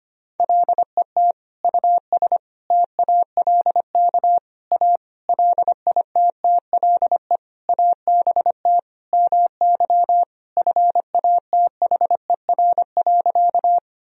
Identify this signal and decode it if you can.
Morse code, heard as a keyed tone, about 25 wpm, 700 Hz. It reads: LET US TALK A LITTLE ABT MY FATHER.
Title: Sentences from Top 400 Words and Abbreviations 25wpm